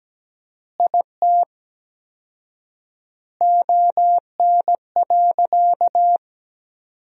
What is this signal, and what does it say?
Morse code, heard as a keyed tone, about 17 wpm, 700 Hz. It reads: IT ON.
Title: 4x Word-Spacing - Sentences from Top 200 Words - 17wpm